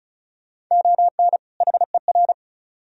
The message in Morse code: ON HER

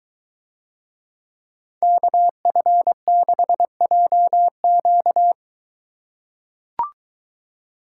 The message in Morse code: KF6JQ

